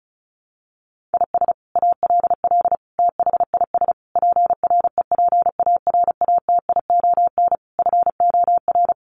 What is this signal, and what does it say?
Morse code, heard as a keyed tone, about 34 wpm, 700 Hz. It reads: IS ALL THIS PREPARATION FOR
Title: Sentences from Top 1400 Words 35wpm